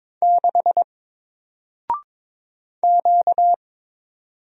Morse code, heard as a keyed tone, about 22 wpm, 700 Hz. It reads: 6 Q
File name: Single-Letter-Number-22wpm